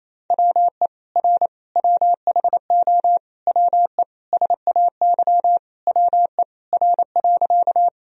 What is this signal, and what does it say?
Morse code, heard as a keyed tone, about 28 wpm, 700 Hz. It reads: WE R WHO WE SAY WE R.
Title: Sentences from Top 200 Words and Abbreviations 28wpm